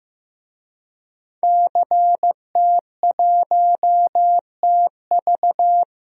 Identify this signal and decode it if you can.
Morse code, heard as a keyed tone, about 15 wpm, 700 Hz. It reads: CT1TV
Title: DXCC-100 Call Signs - 5 Characters - 15wpm